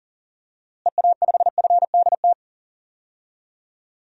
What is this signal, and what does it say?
Morse code, heard as a keyed tone, about 37 wpm, 700 Hz. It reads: EA5FDT
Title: DXCC-100 Call Signs - 6 Characters - Repeated 2x - 40wpm